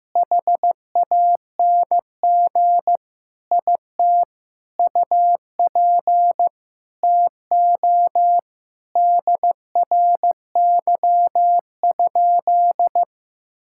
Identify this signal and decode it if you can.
Morse code, heard as a keyed tone, about 15 wpm, 700 Hz. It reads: HANG IT UP TO DRY?
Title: Sentences from Top 1600 Words 15wpm